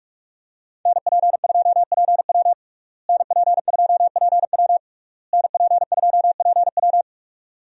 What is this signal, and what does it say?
Morse code, heard as a keyed tone, about 46 wpm, 700 Hz. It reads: NP2PW NP2PW NP2PW
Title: USA Call Signs - 5 Characters - Repeated 3x - 45wpm